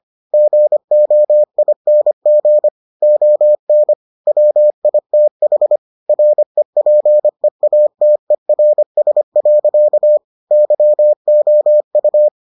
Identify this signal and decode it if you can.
Morse code, heard as a keyed tone, about 25 wpm, 600 Hz. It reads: GOING ON WITH REPEATERS. YOU